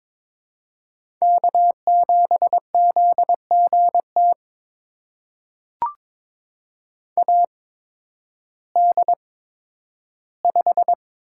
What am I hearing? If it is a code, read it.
Morse code, heard as a keyed tone, about 22 wpm, spaced as for 12 wpm, 700 Hz. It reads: K7ZGT AD5
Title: USA Call Signs - 8x Character Spacing - 22wpm